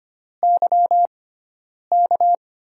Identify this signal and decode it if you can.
Morse code, heard as a keyed tone, about 25 wpm, spaced as for 9 wpm, 700 Hz. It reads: YK
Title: USA Call Signs - 6x Character Spacing - 25wpm